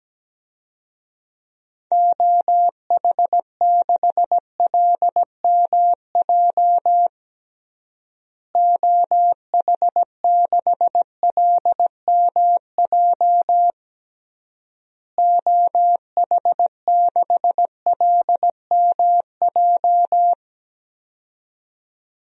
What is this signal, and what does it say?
Morse code, heard as a keyed tone, about 17 wpm, 700 Hz. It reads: OH6LMJ OH6LMJ OH6LMJ
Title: DXCC-100 Call Signs - 6 Characters - Repeated 3x - 17wpm